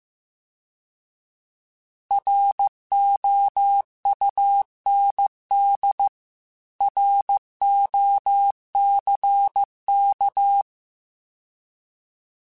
Morse code, heard as a keyed tone, about 15 wpm, 800 Hz. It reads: ROUND ROCK
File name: fromSMs / FromSMs mp3s / Cities@13